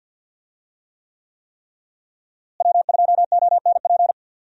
Morse code, heard as a keyed tone, about 52 wpm, 700 Hz. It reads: W2ONP
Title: USA Call Signs - 5 Characters - Repeated 3x - 50wpm